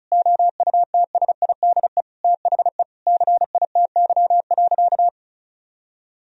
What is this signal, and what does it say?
Morse code, heard as a keyed tone, about 36 wpm, 700 Hz. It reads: OUTSIDE THE CITY.
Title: Sentences from Top 1000 Words - Review Part 4 - 35wpm